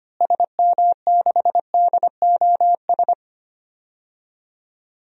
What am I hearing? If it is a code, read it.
Morse code, heard as a keyed tone, about 25 wpm, 700 Hz. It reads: SM6DOS
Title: DXCC-100 Call Signs - 6 Characters - Repeated 2x - 25wpm